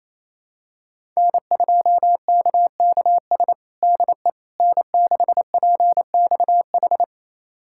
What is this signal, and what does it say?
Morse code, heard as a keyed tone, about 28 wpm, 700 Hz. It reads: N2KKS DE N6PXH
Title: USA Callsign DE Callsign - 5 Characters - 28wpm